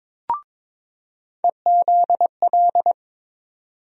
Morse code, heard as a keyed tone, about 21 wpm, 700 Hz. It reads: EZL
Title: USA Call Sign Postfixes - 3 Letters 22wpm